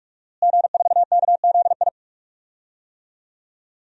Morse code, heard as a keyed tone, about 43 wpm, 700 Hz. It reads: G4KZI
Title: DXCC-100 Call Signs - 5 Characters - Repeated 2x - 45wpm